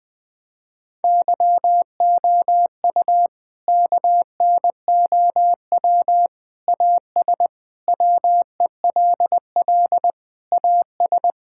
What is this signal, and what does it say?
Morse code, heard as a keyed tone, about 20 wpm, 700 Hz. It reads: YOU KNOW AS WELL AS
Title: Sets of 8 Words - 20wpm